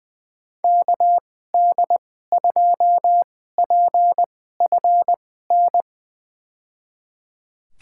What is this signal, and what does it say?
Morse code, heard as a keyed tone, about 20 wpm, spaced as for 15 wpm, 700 Hz. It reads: KD2PFN
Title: USA Call Signs - 6 Characters - 2x Character Spacing - 20wpm